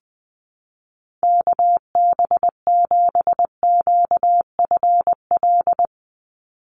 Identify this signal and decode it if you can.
Morse code, heard as a keyed tone, about 20 wpm, 700 Hz. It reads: KB7QFL
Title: USA Call Signs 20wpm